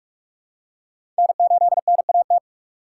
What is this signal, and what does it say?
Morse code, heard as a keyed tone, about 46 wpm, 700 Hz. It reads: N8NAT
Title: USA Call Signs - 4x Character Spacing - 45wpm